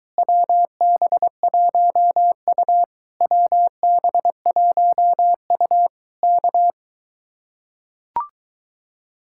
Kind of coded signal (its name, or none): Morse code